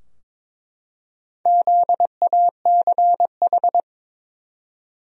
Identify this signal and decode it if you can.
Morse code, heard as a keyed tone, about 22 wpm, 700 Hz. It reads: ZACH
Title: USA Common Names - Max 4 Letters - 22wpm